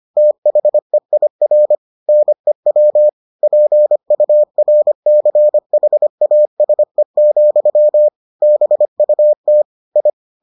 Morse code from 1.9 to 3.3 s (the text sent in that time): NEW